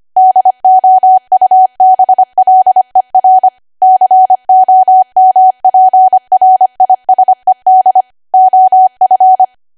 Morse code, heard as a keyed tone, about 25 wpm, 750 Hz. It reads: DOUBLER COMPRISED OF
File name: Archive / 25 WPM / 25WPM